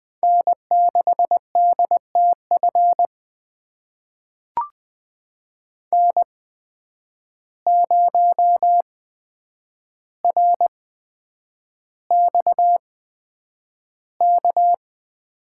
Morse code, heard as a keyed tone, about 21 wpm, spaced as for 9 wpm, 700 Hz. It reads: N6DTF N0RXK
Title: USA Call Signs - 8x Character Spacing - 20wpm